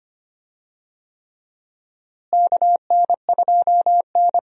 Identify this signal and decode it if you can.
Morse code, heard as a keyed tone, about 25 wpm, 700 Hz. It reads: KN2N
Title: USA Call Signs - 4 Characters - Repeated 3x - 25wpm